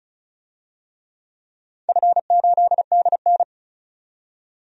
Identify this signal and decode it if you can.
Morse code, heard as a keyed tone, about 36 wpm, 700 Hz. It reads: F8DN
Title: DXCC-100 Call Signs - 4 Characters - 35wpm